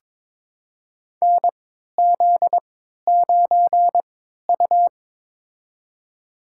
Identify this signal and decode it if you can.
Morse code, heard as a keyed tone, about 22 wpm, spaced as for 12 wpm, 700 Hz. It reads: NZ9U
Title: USA Call Signs - 4 Characters - 3x Character Spacing - 22wpm